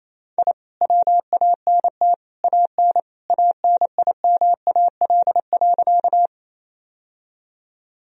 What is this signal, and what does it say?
Morse code, heard as a keyed tone, about 27 wpm, 700 Hz. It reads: I WANT AN ANIMAL.
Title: Sentences from Top 1000 Words - Review Part 1 - 28wpm